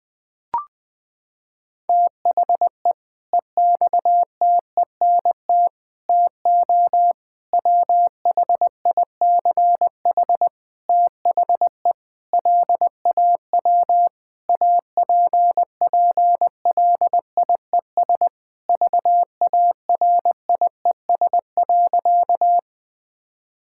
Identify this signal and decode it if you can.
Morse code, heard as a keyed tone, about 20 wpm, 700 Hz. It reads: THE EXTENT TO WHICH THE LAW APPLIES VARIES.
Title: Sentences with 5-Word Phrases - 20wpm